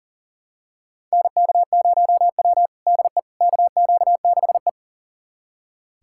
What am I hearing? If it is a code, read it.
Morse code, heard as a keyed tone, about 40 wpm, 700 Hz. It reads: NK0W DE KQ6E
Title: USA Callsign DE Callsign - 4 Characters - 40wpm